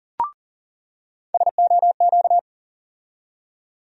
Morse code, heard as a keyed tone, about 40 wpm, 700 Hz. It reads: SOQ
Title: USA Call Sign Postfixes - 3 Letters 40wpm